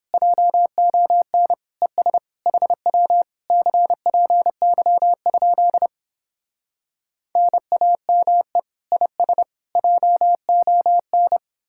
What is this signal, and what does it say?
Morse code, heard as a keyed tone, about 29 wpm, 700 Hz. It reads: JON ES HW CPY? NAME IS JON